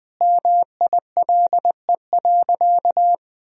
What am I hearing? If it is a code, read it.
Morse code, heard as a keyed tone, about 20 wpm, 700 Hz. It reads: MILE.
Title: Sentences from Top 1000 Words - Review Part 1 - 20wpm